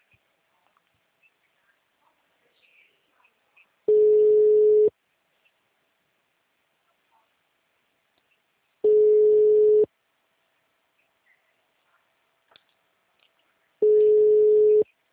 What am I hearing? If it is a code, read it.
Ringback tone (Europe, 425 Hz).